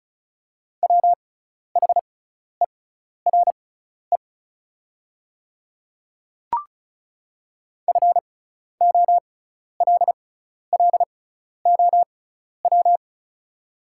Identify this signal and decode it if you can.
Morse code, heard as a keyed tone, about 36 wpm, spaced as for 12 wpm, 690 Hz. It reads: WHERE FOLLOW
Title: 6x Character Spacing - Top 300 Words - 35wpm